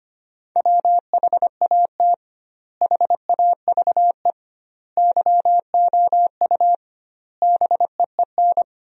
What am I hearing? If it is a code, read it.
Morse code, heard as a keyed tone, about 25 wpm, 700 Hz. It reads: WHAT HAVE YOU BEEN
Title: 2x Word-Spacing - Sentences from Top 200 Words - 25wpm